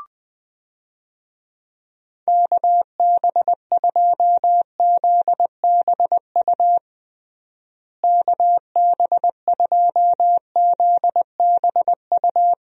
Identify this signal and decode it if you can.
Morse code, heard as a keyed tone, about 20 wpm, 700 Hz. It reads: KB2ZBU KB2ZBU
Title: USA Call Signs - 6 Characters - Repeated 3x - 20wpm